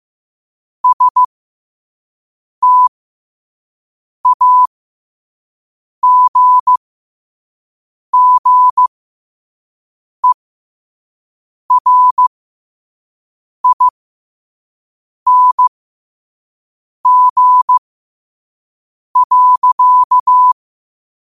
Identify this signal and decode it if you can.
Morse code, heard as a keyed tone, about 15 wpm, spaced as for 5 wpm, 1000 Hz. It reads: STAGGERING.
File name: qotdfiles / 5wpm